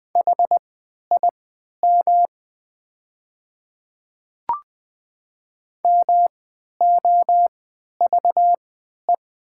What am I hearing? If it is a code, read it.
Morse code, heard as a keyed tone, about 20 wpm, spaced as for 11 wpm, 700 Hz. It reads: HIM MOVE
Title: 3x Character Spacing - Top 200 Words - 20wpm